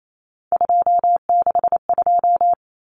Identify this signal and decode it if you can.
Morse code, heard as a keyed tone, about 28 wpm, 700 Hz. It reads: 262
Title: Numbers - 3-Digit 28wpm